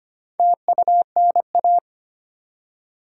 Morse code, heard as a keyed tone, about 25 wpm, 700 Hz. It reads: TUNA
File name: practice / 25wpm / Difficult Words with Hints - 25wpm